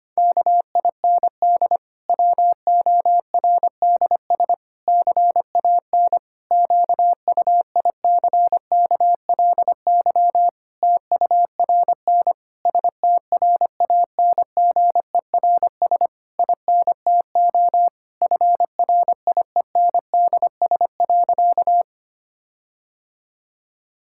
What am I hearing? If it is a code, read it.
Morse code, heard as a keyed tone, about 25 wpm, 700 Hz. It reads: KIND WORDS CAN QUICKLY TURN STRANGERS INTO FRIENDS.
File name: practice / 25wpm / Difficult Sentences with Hints - 25wpm